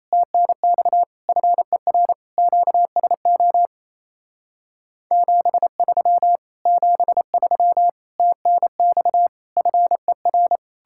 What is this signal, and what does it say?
Morse code, heard as a keyed tone, about 38 wpm, 700 Hz. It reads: TNX FER QSO 73 73 TNX FER